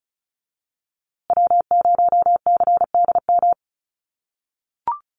Morse code, heard as a keyed tone, about 34 wpm, 700 Hz. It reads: W0CDM
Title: USA Call Signs Repeated 3x 35wpm